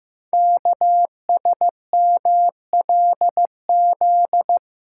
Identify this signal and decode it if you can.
Morse code, heard as a keyed tone, about 15 wpm, 700 Hz. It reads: KSMLZ
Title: Code Groups - 5 Letters - 15wpm